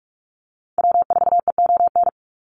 Morse code, heard as a keyed tone, about 44 wpm, 700 Hz. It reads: W4EON
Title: NA SOTA Activator Callsigns 45wpm